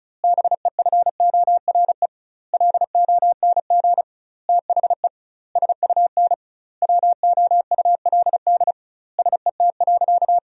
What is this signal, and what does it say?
Morse code, heard as a keyed tone, about 34 wpm, 700 Hz. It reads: BEFORE LONG THE SUN WOULD SET.
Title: 2x Word-Spacing - Sentences from Top 400 Words - 35wpm